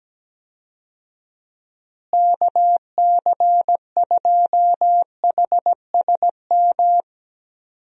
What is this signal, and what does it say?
Morse code, heard as a keyed tone, about 17 wpm, 700 Hz. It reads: KC2HSM
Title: USA Call Signs - 6 Characters - Repeated 2x - 17wpm